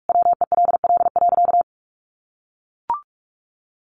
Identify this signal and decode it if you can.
Morse code, heard as a keyed tone, about 44 wpm, 700 Hz. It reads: WELL.
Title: Sentences from Top 1400 Words 45wpm